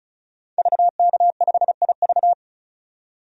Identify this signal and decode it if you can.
Morse code, heard as a keyed tone, about 36 wpm, 700 Hz. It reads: VK5IV
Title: DXCC-100 Call Signs - 5 Characters - 35wpm